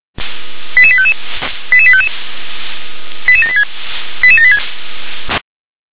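Five-tone selective call (ZVEI), heard as quick runs of five tones, tones of about 70 ms. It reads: 88855 88855 88786 88786
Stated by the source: ZVEI five-tone selective call received with an SDR.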